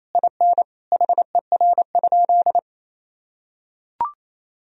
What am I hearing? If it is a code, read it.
Morse code, heard as a keyed tone, about 28 wpm, 700 Hz. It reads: IN HER?
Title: Sentences from Top 1000 Words - Review Part 4 - 28wpm